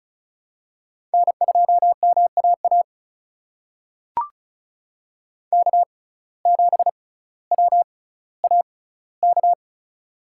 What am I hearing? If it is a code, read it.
Morse code, heard as a keyed tone, about 36 wpm, spaced as for 18 wpm, 700 Hz. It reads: N2MAA K7WAK
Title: USA Call Signs - 5 Characters - 6x Character Spacing - 35wpm